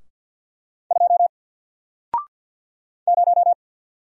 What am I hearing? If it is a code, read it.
Morse code, heard as a keyed tone, about 48 wpm, 700 Hz. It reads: 2 0
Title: Single Letter-Number Alt 50wpm